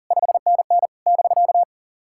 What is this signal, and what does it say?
Morse code, heard as a keyed tone, about 37 wpm, 690 Hz. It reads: 5NN <BK>
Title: SOTA QSO Practice 40wpm